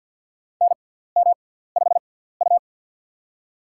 Morse code, heard as a keyed tone, about 52 wpm, spaced as for 17 wpm, 700 Hz. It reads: NM5U
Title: USA Call Signs - 4 Characters - 6x Character Spacing - 50wpm